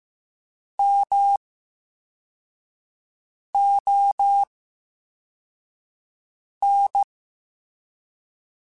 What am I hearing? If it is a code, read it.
Morse code, heard as a keyed tone, about 15 wpm, spaced as for 4 wpm, 800 Hz. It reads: MON
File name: Special INGs / Special INGs-mp3s / Slow ing @13